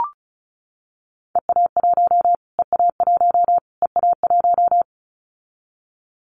Morse code, heard as a keyed tone, about 34 wpm, 700 Hz. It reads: EA1 EA1 EA1
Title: Callsign Prefixes for Top 25 DXCC - Repeated 3x - 35wpm